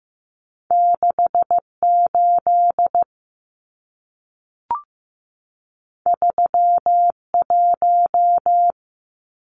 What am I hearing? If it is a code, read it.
Morse code, heard as a keyed tone, about 15 wpm, 700 Hz. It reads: 68 31
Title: Numbers - 2-Digit 15wpm